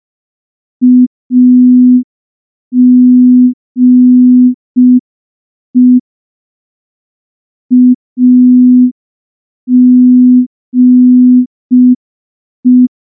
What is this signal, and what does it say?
Morse code, heard as a keyed tone, about 5 wpm, 250 Hz. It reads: AGE AGE
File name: morse / morse-training-05WPM